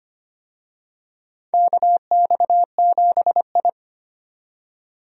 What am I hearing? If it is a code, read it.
Morse code, heard as a keyed tone, about 25 wpm, 700 Hz. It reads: KX7I
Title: USA Call Signs - 4 Characters - 25wpm